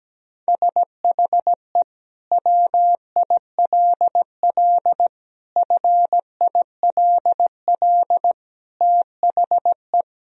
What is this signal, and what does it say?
Morse code, heard as a keyed tone, about 17 wpm, 700 Hz. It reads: SHE WILL FILL THE
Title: Sentences from Top 1000 Words - Review Part 3 - 17wpm